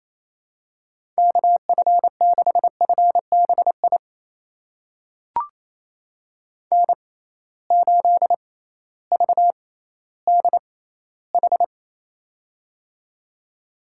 Morse code, heard as a keyed tone, about 28 wpm, spaced as for 15 wpm, 700 Hz. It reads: KF6FBI N8VDH
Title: USA Call Signs - 6x Character Spacing - 28wpm